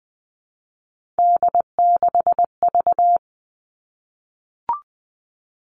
Morse code, heard as a keyed tone, about 20 wpm, 700 Hz. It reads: D6V
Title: Three Challenging Letter-Number Combinations 20wpm